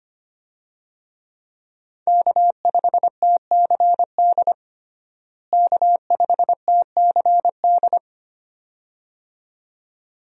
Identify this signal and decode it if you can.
Morse code, heard as a keyed tone, about 25 wpm, 700 Hz. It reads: K5TCD K5TCD
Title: USA Call Signs - 5 Characters - Repeated 2x - 25wpm